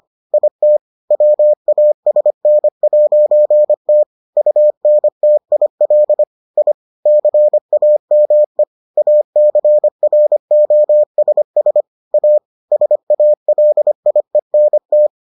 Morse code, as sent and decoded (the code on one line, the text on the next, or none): .. - / .-- .- ... -. .----. - / ..- -. - .. .-.. / .. / -.-. .- -- . / .- -.-. .-. --- ... ... / .- / ... .- .-.. .. -. -
IT WASN'T UNTIL I CAME ACROSS A SALIENT